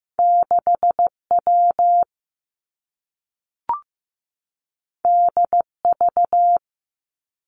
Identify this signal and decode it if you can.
Morse code, heard as a keyed tone, about 15 wpm, 700 Hz. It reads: 6 W D V
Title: Two Challenging Letter-Number Combinations 15wpm